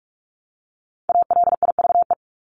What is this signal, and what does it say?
Morse code, heard as a keyed tone, about 44 wpm, 700 Hz. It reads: ALIVE
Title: Top 1751-2000 Words 45wpm